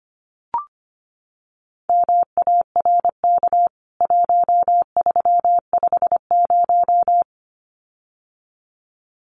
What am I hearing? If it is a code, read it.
Morse code, heard as a keyed tone, about 25 wpm, 700 Hz. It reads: MARK 1350
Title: CWT Exchange 25wpm